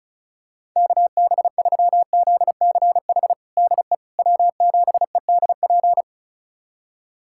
Morse code, heard as a keyed tone, about 34 wpm, 700 Hz. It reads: KB3ZCH DE W7EDP
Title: USA Callsign DE Callsign 35wpm